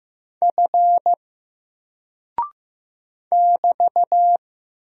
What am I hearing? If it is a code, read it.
Morse code, heard as a keyed tone, about 15 wpm, 700 Hz. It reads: F <BT>
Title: Single Character (Letter:Number:Punctuation:Prosign) 15wpm